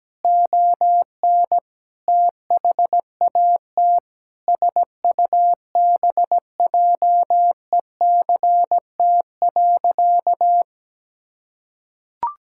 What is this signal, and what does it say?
Morse code, heard as a keyed tone, about 17 wpm, 700 Hz. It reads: ON THAT SUBJECT.
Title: Sentences from Top 1600 Words 17wpm